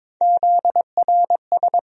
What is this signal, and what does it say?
Morse code, heard as a keyed tone, about 22 wpm, 700 Hz. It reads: ZRS
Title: USA Call Sign Postfixes - 3 Letters 22wpm